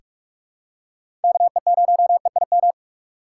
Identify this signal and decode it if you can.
Morse code, heard as a keyed tone, about 44 wpm, 700 Hz. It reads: KE0EIM
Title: USA Call Signs - 6 Characters - 4x Character Spacing - 45wpm